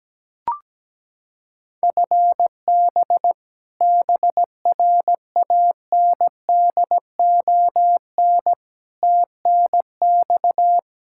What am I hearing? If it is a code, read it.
Morse code, heard as a keyed tone, about 17 wpm, 700 Hz. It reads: FB BRANDON TNX
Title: Practice QSO - Exchange 3 - More Info - 17wpm